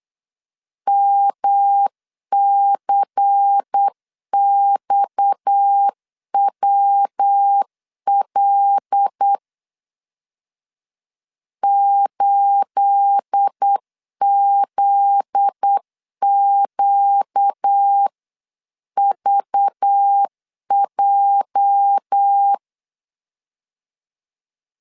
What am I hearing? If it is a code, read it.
Morse code, heard as a keyed tone, about 8 wpm, 800 Hz. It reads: MCXWL 8ZQVJ
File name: CW / DET / 1 9 1500C S A 8wpm 5637